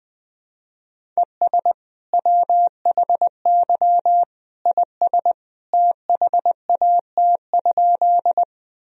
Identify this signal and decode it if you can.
Morse code, heard as a keyed tone, about 21 wpm, 700 Hz. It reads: ES WHY IS THAT?
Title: Sentences from Top 200 Words and Abbreviations 20wpm